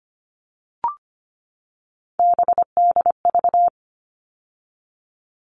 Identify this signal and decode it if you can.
Morse code, heard as a keyed tone, about 25 wpm, 700 Hz. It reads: BDV
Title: All Cut Numbers - 3-Digit 25wpm